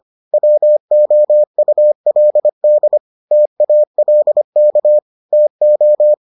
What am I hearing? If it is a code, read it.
Morse code, heard as a keyed tone, about 25 wpm, 600 Hz. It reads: WOULD TALK TO